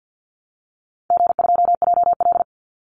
Morse code, heard as a keyed tone, about 50 wpm, 700 Hz. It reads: G2JL
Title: Easy Call Signs - 50wpm